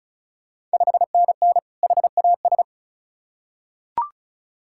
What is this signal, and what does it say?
Morse code, heard as a keyed tone, about 34 wpm, 690 Hz. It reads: 5NN HAS
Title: Ontario QSO Party Exchange 35wpm